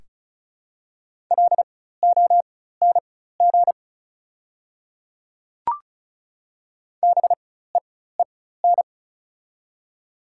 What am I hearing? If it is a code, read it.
Morse code, heard as a keyed tone, about 34 wpm, spaced as for 16 wpm, 690 Hz. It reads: LONG BEEN
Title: 4x Character Spacing - Top 300 Words - 35wpm